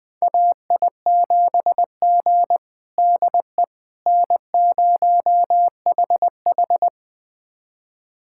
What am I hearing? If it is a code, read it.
Morse code, heard as a keyed tone, about 20 wpm, 700 Hz. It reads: AI7G DE N0HH
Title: USA Callsign DE Callsign - 4 Characters - 20wpm